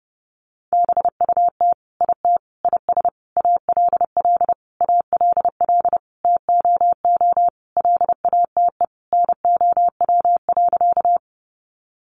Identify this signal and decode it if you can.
Morse code, heard as a keyed tone, about 28 wpm, 700 Hz. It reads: BUT IT IS ALL ALL TOO LATE NOW.
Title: Sentences from Top 300 Words 30wpm